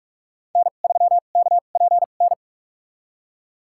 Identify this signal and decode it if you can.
Morse code, heard as a keyed tone, about 46 wpm, spaced as for 33 wpm, 700 Hz. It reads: N3KPN
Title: USA Call Signs - 5 Characters - 2x Character Spacing - 45wpm